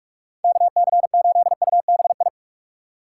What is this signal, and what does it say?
Morse code, heard as a keyed tone, about 46 wpm, 700 Hz. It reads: KC8UBI
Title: USA Call Signs - 6 Characters - 2x Character Spacing - 45wpm